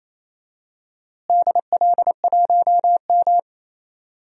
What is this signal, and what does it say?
Morse code, heard as a keyed tone, about 27 wpm, 700 Hz. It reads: DL1M
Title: DXCC-100 Call Signs - 4 Characters - Repeated 2x - 28wpm